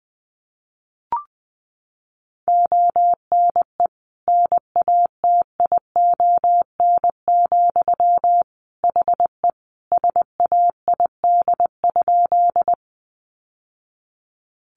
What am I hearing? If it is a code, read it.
Morse code, heard as a keyed tone, about 20 wpm, 700 Hz. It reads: ONE NATION, HE SAID?
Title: Sentences from Top 900 Words 20wpm